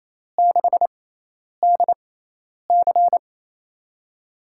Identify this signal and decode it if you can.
Morse code, heard as a keyed tone, about 28 wpm, spaced as for 10 wpm, 700 Hz. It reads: 6DC
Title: USA Call Signs - 5 Characters - 6x Character Spacing - 28wpm